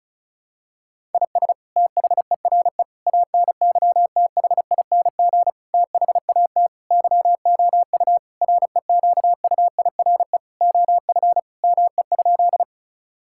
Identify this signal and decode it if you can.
Morse code, heard as a keyed tone, about 36 wpm, 700 Hz. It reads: IS THERE ANYTHING THAT YOU REQUIRE OF ME?
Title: Sentences from Top 1000 Words - Review Part 6 - 35wpm